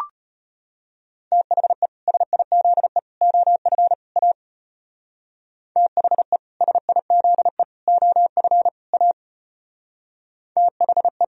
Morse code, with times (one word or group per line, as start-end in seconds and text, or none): 1.3-1.9 THE
2.1-3.0 SIZE
3.2-3.9 OF
4.2-4.3 A
5.8-6.4 THE
6.6-7.6 SIZE
7.9-8.7 OF
8.9-9.1 A
10.6-11.2 THE